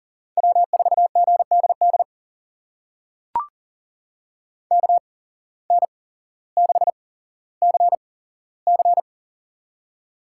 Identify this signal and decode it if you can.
Morse code, heard as a keyed tone, about 38 wpm, spaced as for 17 wpm, 700 Hz. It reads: W4GDD KN6CC
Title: USA Call Signs - 5 Characters - 8x Character Spacing - 40wpm